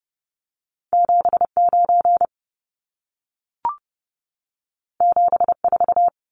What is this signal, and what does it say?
Morse code, heard as a keyed tone, about 30 wpm, 700 Hz. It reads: 79 74
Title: Numbers - 2-Digit 30wpm